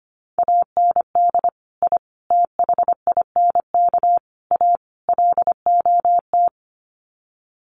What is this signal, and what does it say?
Morse code, heard as a keyed tone, about 25 wpm, 700 Hz. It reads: AND I THINK A LOT
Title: Sets of 5 Words Encore2 25wpm